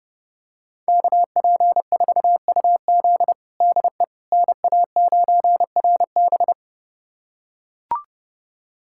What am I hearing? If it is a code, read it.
Morse code, heard as a keyed tone, about 30 wpm, 700 Hz. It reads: KP4UZ DE NA9RB
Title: USA Callsign DE Callsign - 5 Characters - 30wpm